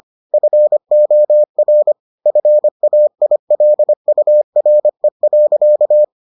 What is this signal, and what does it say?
Morse code, heard as a keyed tone, about 25 wpm, 600 Hz. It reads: FOR FAILURE.